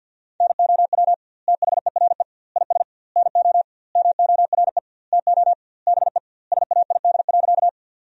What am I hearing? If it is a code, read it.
Morse code, heard as a keyed tone, about 50 wpm, 700 Hz. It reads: NOW THERE IS NO MORE TO BE SAID.
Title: 2x Word-Spacing - Sentences from Top 100 Words - 50wpm